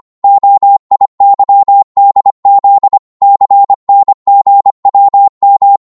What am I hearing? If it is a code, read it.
Morse code, heard as a keyed tone, about 25 wpm, 800 Hz. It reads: OIYDZ CNGWM